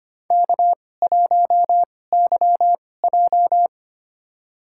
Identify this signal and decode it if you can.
Morse code, heard as a keyed tone, about 25 wpm, spaced as for 18 wpm, 700 Hz. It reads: K1YJ
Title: USA Call Signs - 2x Character Spacing - 25wpm